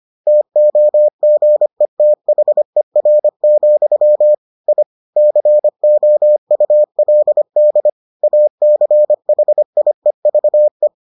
Morse code, heard as a keyed tone, about 25 wpm, 600 Hz. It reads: TOGETHER, I COULD ACHIEVE